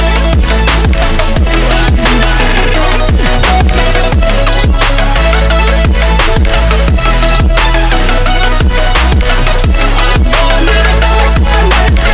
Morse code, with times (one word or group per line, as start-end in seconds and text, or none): none